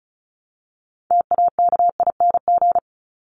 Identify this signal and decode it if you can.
Morse code, heard as a keyed tone, about 34 wpm, 700 Hz. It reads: TAKING
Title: ING Suffix - Top 100 Words with Base Words 35wpm